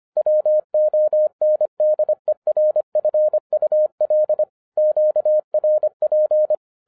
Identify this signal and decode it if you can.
Morse code, heard as a keyed tone, about 25 wpm, 610 Hz. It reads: WONDERFUL QRP
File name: 2011-07-mp3 / n24 QRP OTAKU 25wpm